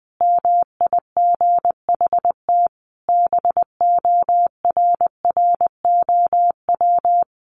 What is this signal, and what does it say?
Morse code, heard as a keyed tone, about 20 wpm, 700 Hz. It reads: MIGHT BORROW
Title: Sentences from Top 1500 Words 20wpm